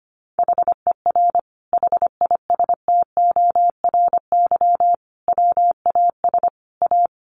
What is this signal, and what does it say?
Morse code, heard as a keyed tone, about 25 wpm, 700 Hz. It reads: HER HISTORY WAS A
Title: Sentences from Top 1300 Words 25wpm